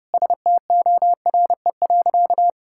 Morse code, heard as a keyed tone, about 30 wpm, 700 Hz. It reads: STORE.
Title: Sentences from Top 1000 Words - Review Part 5 - 30wpm